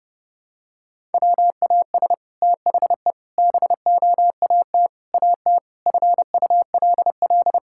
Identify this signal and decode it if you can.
Morse code, heard as a keyed tone, about 30 wpm, 700 Hz. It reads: WAS THE BOAT AT FULL
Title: Sentences from Top 1000 Words - Review Part 5 - 30wpm